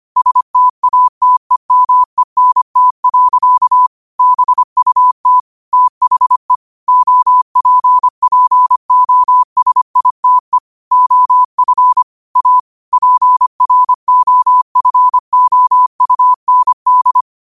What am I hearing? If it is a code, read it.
Morse code, heard as a keyed tone, about 25 wpm, 1000 Hz. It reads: STATEMENT. BUT THE OPPOSITE OF A PROFOUND
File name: qotdfiles / 25wpm